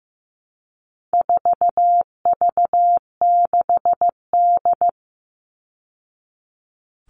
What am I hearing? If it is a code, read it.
Morse code, heard as a keyed tone, about 15 wpm, 700 Hz. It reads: 4V6D
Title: All Cut Numbers - 4-Digit 15wpm